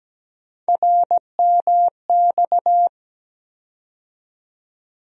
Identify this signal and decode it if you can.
Morse code, heard as a keyed tone, about 17 wpm, 700 Hz. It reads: RMX
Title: USA Call Sign Postfixes - 3 Letters 17wpm